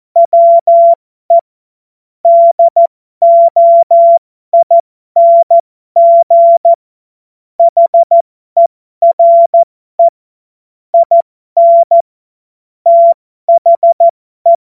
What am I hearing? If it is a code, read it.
Morse code, heard as a keyed tone, about 14 wpm, spaced as for 12 wpm, 680 Hz. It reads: WE DOING HERE IN THE